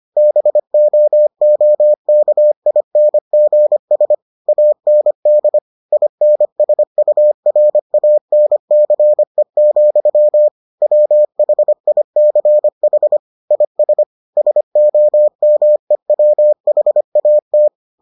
Morse code, heard as a keyed tone, about 25 wpm, 600 Hz. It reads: BOOKINGS AND INSURANCE, WHICH IS SOMEWHAT